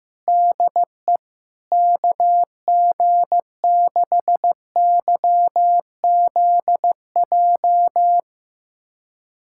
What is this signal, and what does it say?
Morse code, heard as a keyed tone, about 16 wpm, 700 Hz. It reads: DE KG6YZJ
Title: USA Callsign DE Callsign - 6 Characters - 15wpm